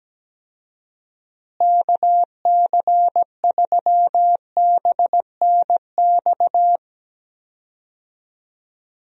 Morse code, heard as a keyed tone, about 17 wpm, 700 Hz. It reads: KC3BNX
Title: USA Call Signs - 6 Characters - Repeated 2x - 17wpm